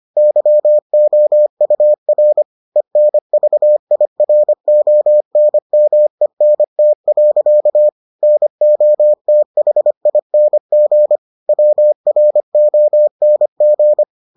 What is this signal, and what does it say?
Morse code, heard as a keyed tone, about 25 wpm, 600 Hz. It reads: YOUR ENVIRONMENT. NOTHING WRONG